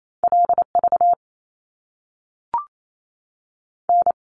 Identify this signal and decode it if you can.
Morse code, heard as a keyed tone, about 28 wpm, 700 Hz. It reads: L V N
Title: Top Two Letter Combinations in Words 28wpm